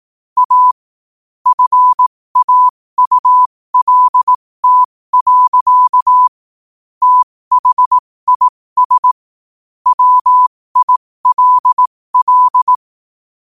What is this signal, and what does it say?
Morse code, heard as a keyed tone, about 18 wpm, spaced as for 15 wpm, 1000 Hz. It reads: A FAULT. THIS WILL
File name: qotdfiles / 15wpm